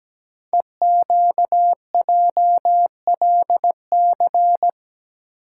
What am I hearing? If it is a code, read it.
Morse code, heard as a keyed tone, about 17 wpm, 700 Hz. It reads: EQJLC
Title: Code Groups - 5 Letter:Numbers - 17wpm